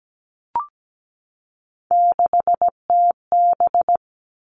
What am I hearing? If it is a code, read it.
Morse code, heard as a keyed tone, about 17 wpm, 700 Hz. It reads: 6TB
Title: Three Challenging Letter-Number Combinations 17wpm